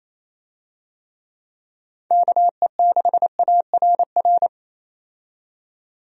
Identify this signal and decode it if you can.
Morse code, heard as a keyed tone, about 28 wpm, 700 Hz. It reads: KE6ARR
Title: USA Call Signs - 6 Characters - 28wpm